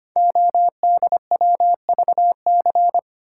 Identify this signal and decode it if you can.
Morse code, heard as a keyed tone, about 25 wpm, 700 Hz. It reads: ODWVC
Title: Code Groups - 5 Letters - 25wpm